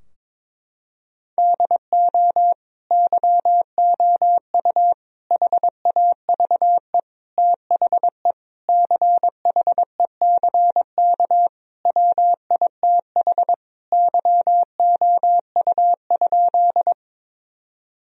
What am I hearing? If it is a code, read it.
Morse code, heard as a keyed tone, about 22 wpm, 700 Hz. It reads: DO YOU HAVE THE CHECK WITH YOU?
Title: Sentences from Top 1000 Words - Review Part 3 - 22wpm